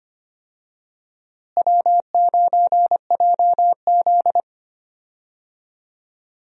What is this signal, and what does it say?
Morse code, heard as a keyed tone, about 25 wpm, 700 Hz. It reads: W9JZ
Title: USA Call Signs - 4 Characters - Repeated 2x - 25wpm